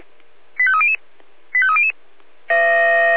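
Five-tone selective call (ZVEI), heard as quick runs of five tones, tones of about 70 ms. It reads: 85300 85300
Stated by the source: ZVEI five-tone call received via SDR.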